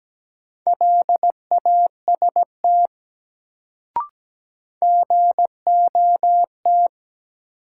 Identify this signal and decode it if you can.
Morse code, heard as a keyed tone, about 17 wpm, 700 Hz. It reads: LAST GOT
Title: Sentences from Top 300 Words - New Word Drill - 17wpm